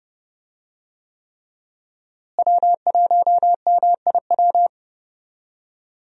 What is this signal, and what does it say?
Morse code, heard as a keyed tone, about 30 wpm, 700 Hz. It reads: W1MIW
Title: USA Call Signs - 5 Characters - Repeated 3x - 30wpm